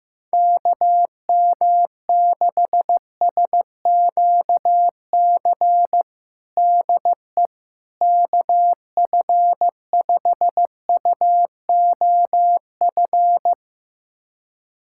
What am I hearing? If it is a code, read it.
Morse code, heard as a keyed tone, about 15 wpm, 700 Hz. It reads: KM6SQC DE KF5UOF
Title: USA Callsign DE Callsign - 6 Characters - 15wpm